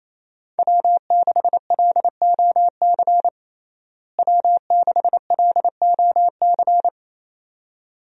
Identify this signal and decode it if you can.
Morse code, heard as a keyed tone, about 28 wpm, 700 Hz. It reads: W6LOC W6LOC
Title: USA Call Signs - 5 Characters - Repeated 2x - 28wpm